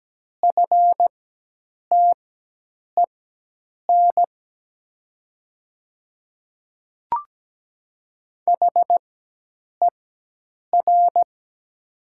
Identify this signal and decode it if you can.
Morse code, heard as a keyed tone, about 17 wpm, spaced as for 8 wpm, 700 Hz. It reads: FTEN HER
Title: 4x Character Spacing - Top 300 Words - 17wpm